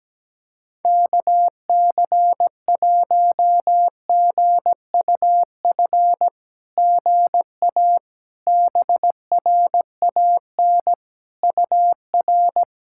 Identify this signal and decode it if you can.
Morse code, heard as a keyed tone, about 17 wpm, 660 Hz. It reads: KC1GUF GA BRAN UR
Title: SOTA QSO Practice 17wpm